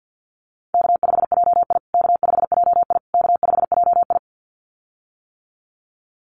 Morse code, heard as a keyed tone, about 50 wpm, 700 Hz. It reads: K5JI K5JI K5JI
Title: USA Call Signs Repeated 3x 50wpm